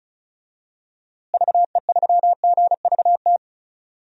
Morse code, heard as a keyed tone, about 36 wpm, 700 Hz. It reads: VE3GVT
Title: DXCC-100 Call Signs - 6 Characters - Repeated 3x - 35wpm